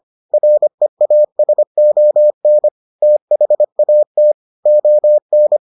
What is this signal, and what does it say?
Morse code, heard as a keyed tone, about 25 wpm, 600 Hz. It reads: REASON THAT ON